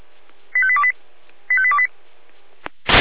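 Five-tone selective call (ZVEI), heard as quick runs of five tones, tones of about 70 ms. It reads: 85828 85828
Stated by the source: ZVEI five-tone selective call 85820 received via SDR.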